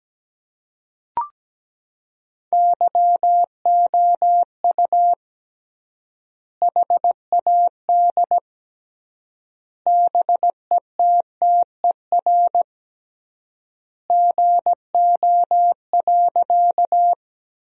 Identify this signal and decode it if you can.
Morse code, heard as a keyed tone, about 17 wpm, 700 Hz. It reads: YOU HAD BETTER GO.
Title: 3x Word-Spacing - Sentences from Top 400 Words - 17wpm